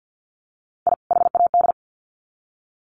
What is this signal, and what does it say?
Morse code, heard as a keyed tone, about 50 wpm, 700 Hz. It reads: I HAD
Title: Sets of 2 Words 50wpm